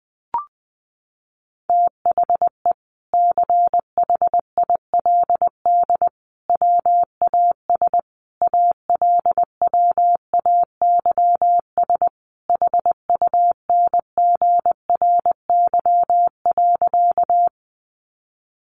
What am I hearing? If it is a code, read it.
Morse code, heard as a keyed tone, about 20 wpm, 700 Hz. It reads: THE CHILD WAS ALWAYS HUNGRY.
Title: Sentences from Top 1400 Words 20wpm